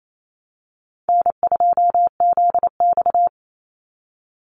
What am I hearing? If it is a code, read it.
Morse code, heard as a keyed tone, about 28 wpm, 700 Hz. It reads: N2ZX
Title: Call Signs Repeated 3x - 28wpm